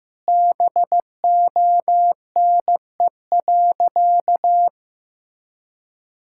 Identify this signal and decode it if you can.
Morse code, heard as a keyed tone, about 15 wpm, 700 Hz. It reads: BONE.
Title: Sentences from Top 1600 Words 15wpm